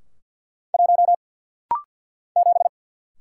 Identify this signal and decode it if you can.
Morse code, heard as a keyed tone, about 50 wpm, 690 Hz. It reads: 1 7
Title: Single Letter-Number Alt - Rapid-Fire 50wpm